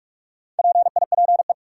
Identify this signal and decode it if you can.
Morse code, heard as a keyed tone, about 46 wpm, 700 Hz. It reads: PIPE
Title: Sentences from Top 1500 Words - New Word Drill - 45wpm